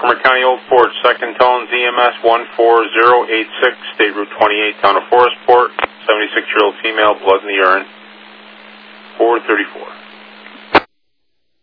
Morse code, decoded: none

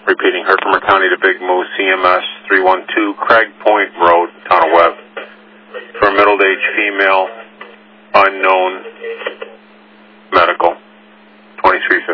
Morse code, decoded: none

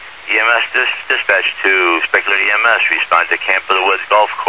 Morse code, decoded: none

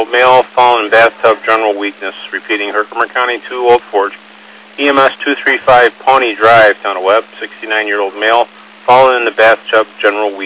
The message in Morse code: none

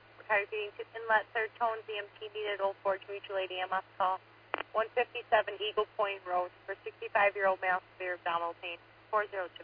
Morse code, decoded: none

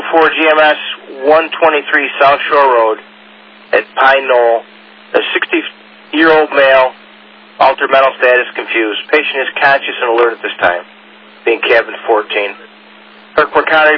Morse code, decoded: none

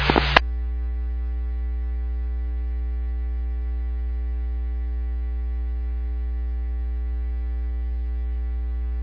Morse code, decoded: none

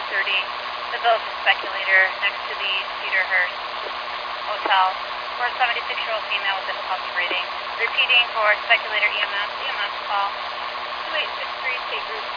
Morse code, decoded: none

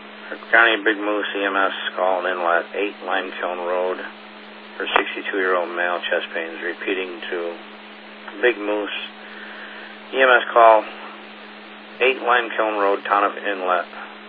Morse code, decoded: none

